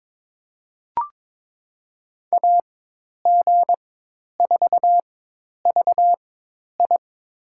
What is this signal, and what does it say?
Morse code, heard as a keyed tone, about 22 wpm, spaced as for 10 wpm, 700 Hz. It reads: AG4VI
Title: USA Call Signs - 5 Characters - 4x Character Spacing - 22wpm